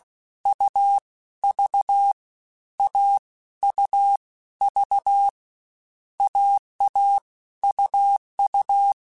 Morse code, heard as a keyed tone, about 16 wpm, 800 Hz. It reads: U V A U V AA UU